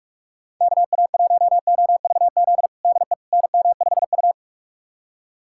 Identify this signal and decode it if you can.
Morse code, heard as a keyed tone, about 44 wpm, 700 Hz. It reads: KA1OVZ DE NM5U
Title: USA Callsign DE Callsign 45wpm